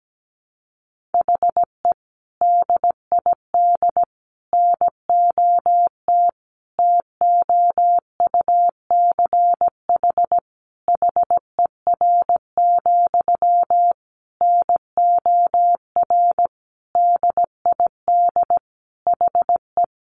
Morse code, heard as a keyed tone, about 17 wpm, 700 Hz. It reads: HE DID NOT TOUCH HER, NOR DID HE
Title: Sentences from Top 1500 Words 17wpm